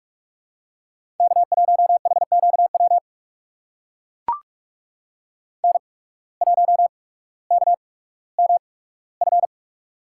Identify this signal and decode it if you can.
Morse code, heard as a keyed tone, about 44 wpm, spaced as for 19 wpm, 690 Hz. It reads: K1HQW N1KMF
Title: USA Call Signs - 5 Characters - 8x Character Spacing - 45wpm